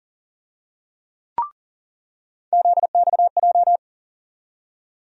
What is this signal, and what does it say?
Morse code, heard as a keyed tone, about 40 wpm, 700 Hz. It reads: ZXJ
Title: USA Call Sign Postfixes - 3 Letters 40wpm